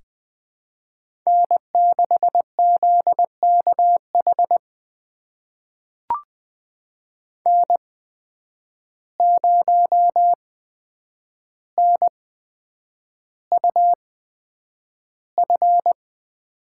Morse code, heard as a keyed tone, about 21 wpm, spaced as for 9 wpm, 700 Hz. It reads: N6ZKH N0NUF
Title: USA Call Signs - 5 Characters - 8x Character Spacing - 20wpm